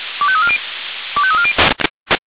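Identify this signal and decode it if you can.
Five-tone selective call (ZVEI), heard as quick runs of five tones, tones of about 70 ms. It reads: 25340 25340
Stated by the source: Five-tone ZVEI selective call received on SDR.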